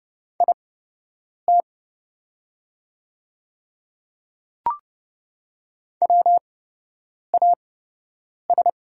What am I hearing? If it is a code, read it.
Morse code, heard as a keyed tone, about 30 wpm, spaced as for 8 wpm, 700 Hz. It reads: IT WAS
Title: 8x Character Spacing - Top 100 Words - 30wpm